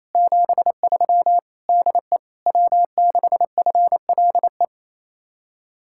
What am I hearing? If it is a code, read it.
Morse code, heard as a keyed tone, about 37 wpm, 700 Hz. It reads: 73 DE W6FLE